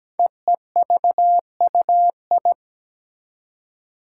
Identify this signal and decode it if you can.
Morse code, heard as a keyed tone, about 17 wpm, 700 Hz. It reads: EEVUI
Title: Code Groups - 5 Letters - 17wpm